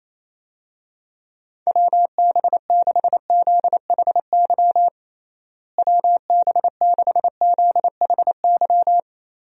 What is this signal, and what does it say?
Morse code, heard as a keyed tone, about 28 wpm, 700 Hz. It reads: WB6ZHY WB6ZHY
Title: USA Call Signs - 6 Characters - Repeated 3x - 28wpm